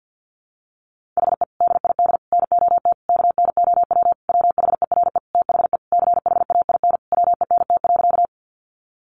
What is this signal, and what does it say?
Morse code, heard as a keyed tone, about 50 wpm, 700 Hz. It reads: HE DID NOT KNOW WHERE THE CHAIN WENT.
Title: Sentences from Top 1500 Words 50wpm